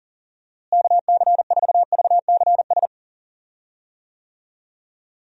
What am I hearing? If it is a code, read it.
Morse code, heard as a keyed tone, about 40 wpm, 700 Hz. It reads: KC4VCS
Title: USA Call Signs - 6 Characters - Repeated 2x - 40wpm